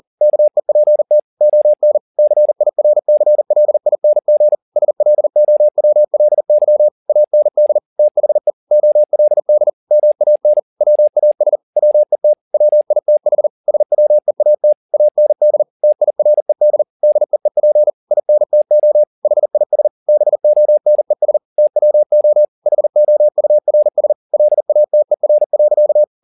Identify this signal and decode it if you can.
Morse code, heard as a keyed tone, about 40 wpm, 600 Hz. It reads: KEPT ON CIRCLING SLOWLY AND THE OLD MAN WAS WET WITH SWEAT AND TIRED DEEP INTO HIS BONES TWO HOURS LATER.
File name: the old man and the sea 2 40wpm